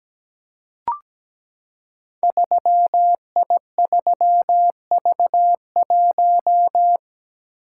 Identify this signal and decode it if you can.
Morse code, heard as a keyed tone, about 17 wpm, 700 Hz. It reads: 3I3V1
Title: Code Groups - 5 Letter:Numbers - 17wpm